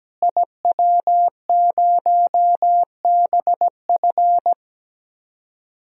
Morse code, heard as a keyed tone, about 17 wpm, 700 Hz. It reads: IW0BF
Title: DXCC-100 Call Signs - 5 Characters - Repeated 2x - 17wpm